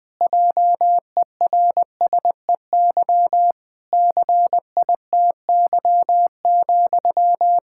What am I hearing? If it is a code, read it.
Morse code, heard as a keyed tone, about 21 wpm, 700 Hz. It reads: JERSEY CITY,
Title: Practice QSO - Exchange 1 - Beginning - 20wpm